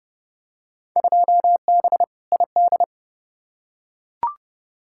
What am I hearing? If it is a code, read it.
Morse code, heard as a keyed tone, about 30 wpm, 700 Hz. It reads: 2B ID
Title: Field Day Exchanges Encore - 30wpm